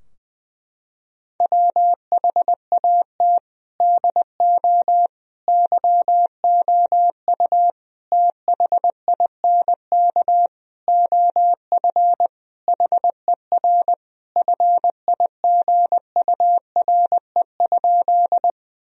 Morse code, heard as a keyed tone, about 21 wpm, 700 Hz. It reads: WHAT DO YOU THINK OF HER FIGURE?
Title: Sentences from Top 1000 Words - Review Part 2 - 20wpm